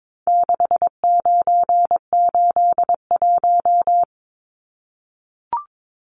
Morse code, heard as a keyed tone, about 22 wpm, 700 Hz. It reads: 6981
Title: Numbers - 4-Digit 22wpm